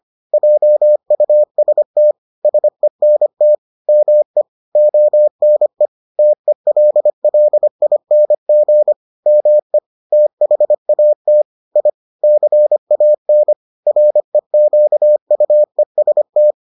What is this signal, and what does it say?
Morse code, heard as a keyed tone, about 25 wpm, 600 Hz. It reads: JUST SENT ME ONE TELLING ME THAT I CAN REQUEST